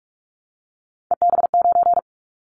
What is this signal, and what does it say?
Morse code, heard as a keyed tone, about 46 wpm, 700 Hz. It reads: EB9
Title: Callsign Prefixes for Top 100 DXCC - 45wpm